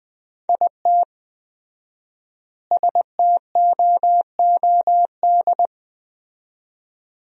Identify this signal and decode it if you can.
Morse code, heard as a keyed tone, about 20 wpm, 700 Hz. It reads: IT STOOD
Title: 4x Word-Spacing - Sentences from Top 500 Words - 20wpm